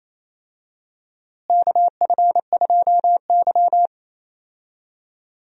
Morse code, heard as a keyed tone, about 28 wpm, 700 Hz. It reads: KF2Y
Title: USA Call Signs - 4 Characters - Repeated 2x - 28wpm